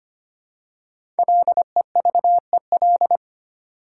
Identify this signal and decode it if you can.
Morse code, heard as a keyed tone, about 25 wpm, 700 Hz. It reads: LEVEL
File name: 5 Letter Words - First 3 Letters Given - 25wpm